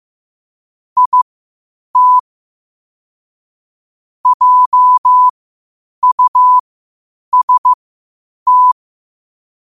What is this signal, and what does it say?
Morse code, heard as a keyed tone, about 15 wpm, spaced as for 8 wpm, 1000 Hz. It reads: IT JUST